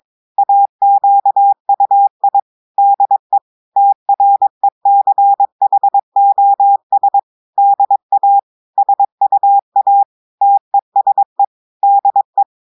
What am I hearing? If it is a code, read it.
Morse code, heard as a keyed tone, about 22 wpm, 800 Hz. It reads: AQUI DE TRECHOS DA SUA TESE DE
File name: audios / portugais / portugais texte en clair 1 vitesse 22 mots minutes